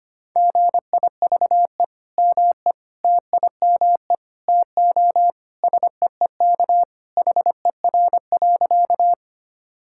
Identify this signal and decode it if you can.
Morse code, heard as a keyed tone, about 25 wpm, 700 Hz. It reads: GIVE ME TIME TO SEEK HER.
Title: Sentences from Top 1600 Words 25wpm